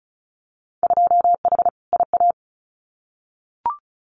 Morse code, heard as a keyed tone, about 36 wpm, 700 Hz. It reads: 2H IA
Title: Winter Field Day Exchanges Encore 35wpm